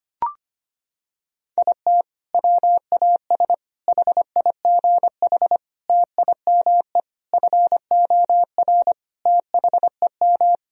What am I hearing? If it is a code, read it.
Morse code, heard as a keyed tone, about 25 wpm, 700 Hz. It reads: IT WAS HIGH TIME FOR THEM
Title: Sentences from Top 1000 Words - Review Part 2 - 25wpm